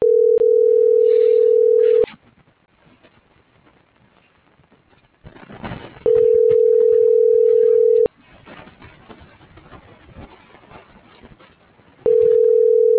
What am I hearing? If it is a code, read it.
Ringback tone (North America), 440 + 480 Hz.